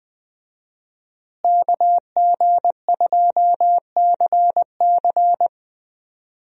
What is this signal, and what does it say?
Morse code, heard as a keyed tone, about 21 wpm, 700 Hz. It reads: KG2CC
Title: USA Call Signs - 5 Characters - Repeated 2x - 20wpm